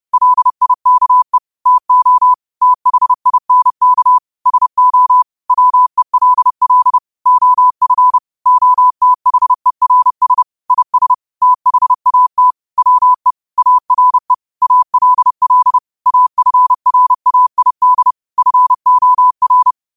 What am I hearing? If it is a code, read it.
Morse code, heard as a keyed tone, about 30 wpm, 1000 Hz. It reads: LIKE TO THINK SO WELL OF OTHERS IS THAT WE ARE ALL AFRAID FOR
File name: qotdfiles / 30wpm